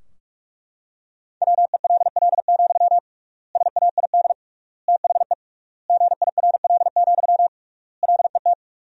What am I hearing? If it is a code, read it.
Morse code, heard as a keyed tone, about 43 wpm, 700 Hz. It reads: WELL, SAID THE GIRL, LET
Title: 3x Word-Spacing - Sentences from Top 400 Words - 45wpm